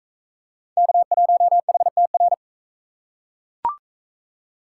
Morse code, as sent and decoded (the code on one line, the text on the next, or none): -.- .---- .... - .-.
K1HTR